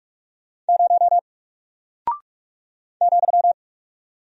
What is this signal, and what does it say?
Morse code, heard as a keyed tone, about 44 wpm, 700 Hz. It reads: 0 ,
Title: Single Character (Letter:Number:Punctuation:Prosign) Alt 45wpm